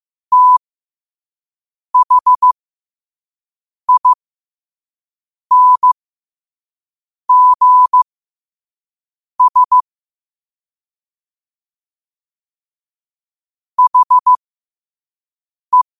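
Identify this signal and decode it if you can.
Morse code, heard as a keyed tone, about 15 wpm, spaced as for 5 wpm, 1000 Hz. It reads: THINGS HE
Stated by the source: Morse code quote of the day at 5 wpm.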